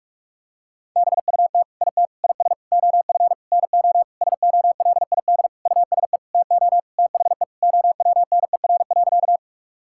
Morse code, heard as a keyed tone, about 46 wpm, 700 Hz. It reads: BUT IT IS OF NO SOLID USE TO THE OWNER.
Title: Sentences from Top 1600 Words 45wpm